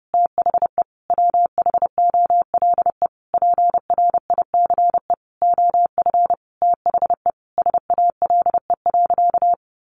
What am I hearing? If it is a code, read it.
Morse code, heard as a keyed tone, about 30 wpm, 700 Hz. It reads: THE WHOLE PRICE OF THE SALE.
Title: Sentences from Top 1400 Words 30wpm